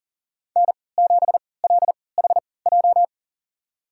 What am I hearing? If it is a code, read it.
Morse code, heard as a keyed tone, about 40 wpm, spaced as for 23 wpm, 700 Hz. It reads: N7LHJ
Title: USA Call Signs - 5 Characters - 3x Character Spacing - 40wpm